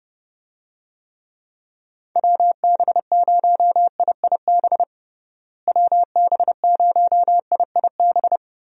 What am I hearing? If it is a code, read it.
Morse code, heard as a keyed tone, about 30 wpm, 700 Hz. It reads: WB0IIB WB0IIB
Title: USA Call Signs - 6 Characters - Repeated 3x - 30wpm